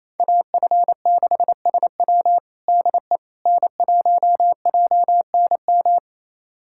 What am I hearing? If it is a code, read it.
Morse code, heard as a keyed tone, about 27 wpm, 700 Hz. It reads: AF6SW DE N1JNM
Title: USA Callsign DE Callsign - 5 Characters - 28wpm